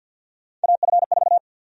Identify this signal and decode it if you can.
Morse code, heard as a keyed tone, about 49 wpm, 700 Hz. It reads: AF4
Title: USA Call Sign Prefixes - 50wpm